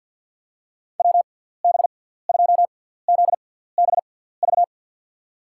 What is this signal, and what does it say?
Morse code, heard as a keyed tone, about 47 wpm, spaced as for 17 wpm, 700 Hz. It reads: WB2ZBV
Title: USA Call Signs - 6 Characters - 6x Character Spacing - 50wpm